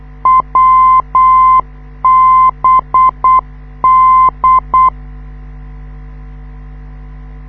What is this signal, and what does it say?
Morse code, heard as a keyed tone, about 8 wpm, 1040 Hz. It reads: WBD
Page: this is how the NDB of the Wiesbaden US Army Airfield at Erbenheim sounds like, when you live 2 km away